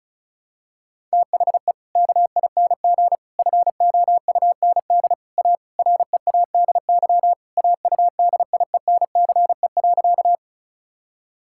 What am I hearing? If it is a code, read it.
Morse code, heard as a keyed tone, about 36 wpm, 700 Hz. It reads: THE KING FOUND A READY AUDIENCE.
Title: Sentences from Top 1000 Words - Review Part 5 - 35wpm